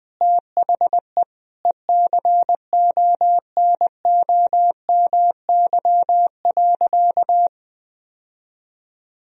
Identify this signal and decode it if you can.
Morse code, heard as a keyed tone, about 21 wpm, 700 Hz. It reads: THE ECONOMY.
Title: Sentences from Top 1000 Words - Review Part 3 - 20wpm